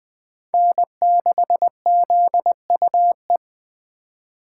Morse code, heard as a keyed tone, about 24 wpm, 700 Hz. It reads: N6ZUE
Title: Speed Racing - USA Call Signs 20wpm